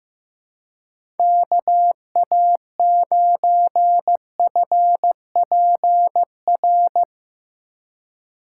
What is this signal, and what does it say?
Morse code, heard as a keyed tone, about 15 wpm, 700 Hz. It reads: KA9FPR
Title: USA Call Signs - 6 Characters - 3x Character Spacing - 15wpm